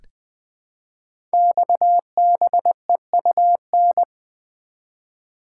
Morse code, heard as a keyed tone, about 20 wpm, 700 Hz. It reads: XBEUN
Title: Code Groups - 5 Letters - 20wpm